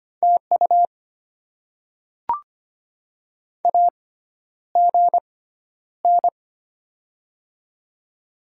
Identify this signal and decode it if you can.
Morse code, heard as a keyed tone, about 25 wpm, spaced as for 11 wpm, 700 Hz. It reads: TU AGN
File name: Top 20 QSO Elements - 6x Character Spacing - 25wpm